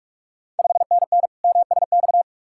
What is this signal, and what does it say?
Morse code, heard as a keyed tone, about 45 wpm, 700 Hz. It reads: 5NN MSX
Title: Ontario QSO Party Exchange 45wpm